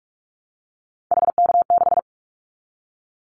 Morse code, heard as a keyed tone, about 44 wpm, 700 Hz. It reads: HK6
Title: Callsign Prefixes for Top 100 DXCC - 45wpm